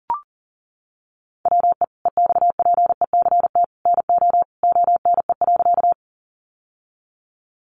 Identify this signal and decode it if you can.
Morse code, heard as a keyed tone, about 40 wpm, 700 Hz. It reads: WE EXPECT NO ONE.
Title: Sentences from Top 1100 Words 40wpm